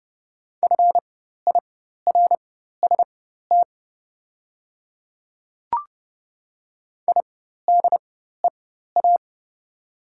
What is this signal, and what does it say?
Morse code, heard as a keyed tone, about 30 wpm, spaced as for 14 wpm, 690 Hz. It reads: FIRST IDEA
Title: 4x Character Spacing - Top 500 Words - 30wpm